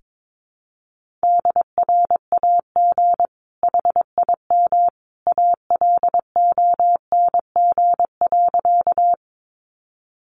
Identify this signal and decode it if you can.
Morse code, heard as a keyed tone, about 22 wpm, 700 Hz. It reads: DRAG HIM ALONG.
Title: Sentences from Top 1500 Words 22wpm